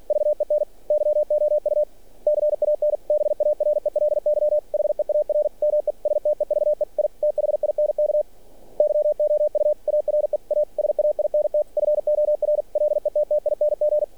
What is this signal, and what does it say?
Morse code, heard as a keyed tone, about 45 wpm, 600 Hz. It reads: VEN YOU CAN BARELY HEAR ME STEVE I THINK YOU ARE A SAINT FOR LETTING
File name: archived / CRSnet